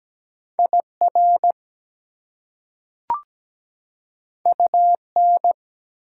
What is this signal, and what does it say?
Morse code, heard as a keyed tone, about 17 wpm, 700 Hz. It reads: IR UN
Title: Prefixes and Suffixes 17wpm